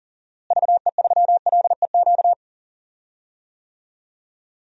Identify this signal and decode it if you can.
Morse code, heard as a keyed tone, about 37 wpm, 700 Hz. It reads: VE3LEQ
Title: DXCC-100 Call Signs - 6 Characters - Repeated 3x - 40wpm